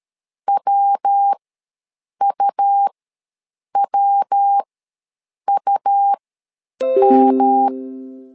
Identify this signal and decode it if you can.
Morse code, heard as a keyed tone, about 13 wpm, spaced as for 7 wpm, 800 Hz. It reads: WUWUM
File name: LEZIONI / 3 MWRO / 3 MW ws 12wpm 1816